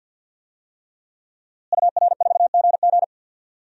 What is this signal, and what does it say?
Morse code, heard as a keyed tone, about 47 wpm, 700 Hz. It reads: UR4GG
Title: DXCC-100 Call Signs - 5 Characters - Repeated 3x - 50wpm